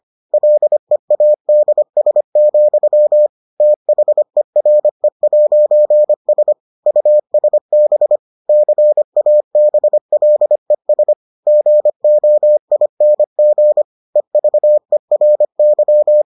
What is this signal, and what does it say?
Morse code, heard as a keyed tone, about 25 wpm, 600 Hz. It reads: LEADS, THERE'S USB CABLES GOING EVERY